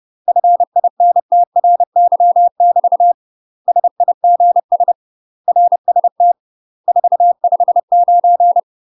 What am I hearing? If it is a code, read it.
Morse code, heard as a keyed tone, about 30 wpm, 700 Hz. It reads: FINTRY <BT> SIGS RST 459
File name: UK MASTER QSOs TRAINING / 30WPM / UK MASTER QSOs 0043